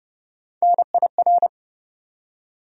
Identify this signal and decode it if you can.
Morse code, heard as a keyed tone, about 30 wpm, 700 Hz. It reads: NIR
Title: USA Call Sign Postfixes - 3 Letters 30wpm